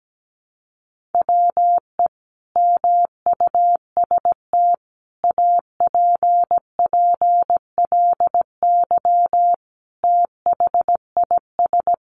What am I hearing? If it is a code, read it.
Morse code, heard as a keyed tone, about 17 wpm, 700 Hz. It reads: WE MUST APPLY THIS
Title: Sentences from Top 1400 Words 17wpm